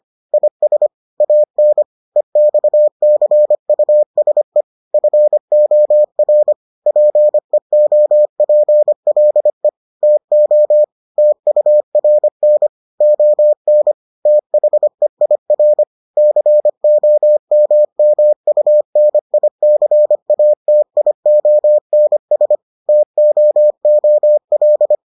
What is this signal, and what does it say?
Morse code, heard as a keyed tone, about 25 wpm, 600 Hz. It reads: IS AN EXCUSE FOR PEOPLE TO TURN ON THEIR COMMUNICATIONS TOOL